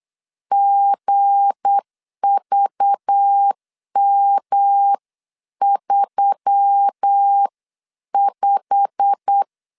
Morse code, heard as a keyed tone, about 8 wpm, 800 Hz. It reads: GVM35
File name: CW / DET / 1 9 1500C S B 8wpm 5624